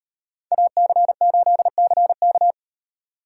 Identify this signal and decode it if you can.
Morse code, heard as a keyed tone, about 38 wpm, 700 Hz. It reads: AC8CK